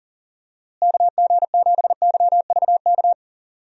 Morse code, heard as a keyed tone, about 38 wpm, 700 Hz. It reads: KG7YVK
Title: USA Call Signs - 6 Characters - Repeated 2x - 40wpm